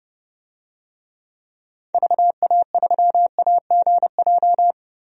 Morse code, heard as a keyed tone, about 32 wpm, 700 Hz. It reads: VA3AGJ
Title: DXCC-100 Call Signs - 6 Characters - Repeated 3x - 30wpm